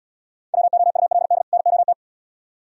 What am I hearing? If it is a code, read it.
Morse code, heard as a keyed tone, about 76 wpm, 700 Hz. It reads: WOULD NOT